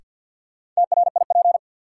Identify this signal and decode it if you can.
Morse code, heard as a keyed tone, about 52 wpm, 700 Hz. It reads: TRIP
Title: Sentences from Top 1000 Words - Review Part 4 - New Word Drill - 50wpm